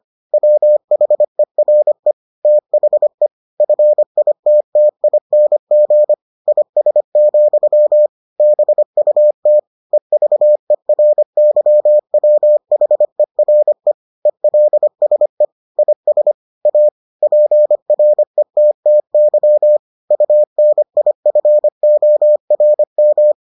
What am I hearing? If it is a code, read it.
Morse code, heard as a keyed tone, about 25 wpm, 600 Hz. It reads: WHERE THE FITTING IS, BUT EVERYWHERE ELSE IS A PRETTY UNIFORM